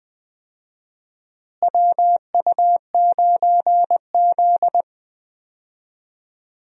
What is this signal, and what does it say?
Morse code, heard as a keyed tone, about 20 wpm, 700 Hz. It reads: WU9Z
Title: USA Call Signs - 4 Characters - Repeated 3x - 20wpm